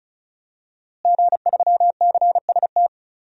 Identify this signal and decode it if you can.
Morse code, heard as a keyed tone, about 36 wpm, 700 Hz. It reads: G3CST
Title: DXCC-100 Call Signs - 5 Characters - Repeated 3x - 35wpm